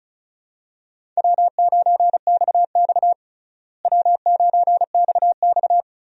Morse code, heard as a keyed tone, about 36 wpm, 700 Hz. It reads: W9XX W9XX
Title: USA Call Signs - 4 Characters - Repeated 3x - 35wpm